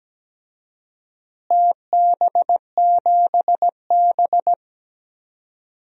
Morse code, heard as a keyed tone, about 17 wpm, 700 Hz. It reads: TB7B
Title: DXCC-100 Call Signs - 4 Characters - Repeated 2x - 17wpm